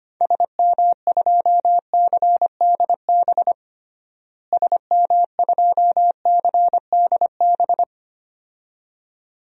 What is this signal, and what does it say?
Morse code, heard as a keyed tone, about 25 wpm, 700 Hz. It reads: SM2CDB SM2CDB
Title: DXCC-100 Call Signs - 6 Characters - Repeated 2x - 25wpm